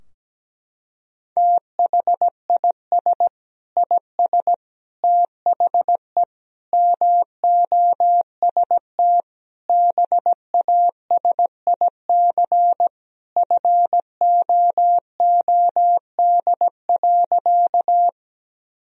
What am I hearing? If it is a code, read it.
Morse code, heard as a keyed tone, about 17 wpm, 700 Hz. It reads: THIS IS THE MOST BASIC FOOD.
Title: Sentences from Top 1000 Words - Review Part 3 - 17wpm